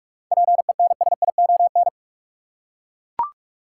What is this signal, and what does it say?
Morse code, heard as a keyed tone, about 42 wpm, 700 Hz. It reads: PENSION
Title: Seven Letter Words - 45wpm